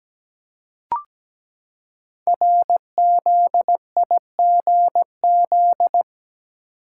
Morse code, heard as a keyed tone, about 17 wpm, 700 Hz. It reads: RZIGZ
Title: Code Groups - 5 Letters - 17wpm